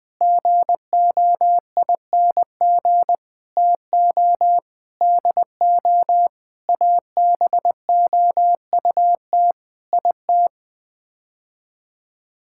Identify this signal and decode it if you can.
Morse code, heard as a keyed tone, about 20 wpm, 700 Hz. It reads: GOING TO DO ABOUT IT
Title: Sets of 8 Words - 20wpm